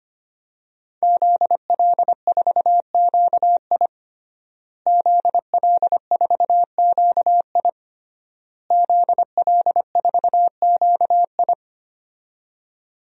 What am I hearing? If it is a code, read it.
Morse code, heard as a keyed tone, about 25 wpm, 700 Hz. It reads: ZL4QI ZL4QI ZL4QI
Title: DXCC-100 Call Signs - 5 Characters - Repeated 3x - 25wpm